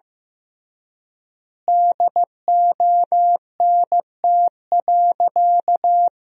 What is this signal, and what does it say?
Morse code, heard as a keyed tone, about 16 wpm, 700 Hz. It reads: DONT.
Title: 3x Word-Spacing - Sentences from Top 300 Words - 15wpm